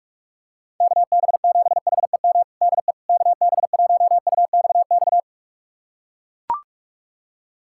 Morse code, heard as a keyed tone, about 42 wpm, 700 Hz. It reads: KB7HEM DE KB1UXX
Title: USA Callsign DE Callsign - 6 Characters - 45wpm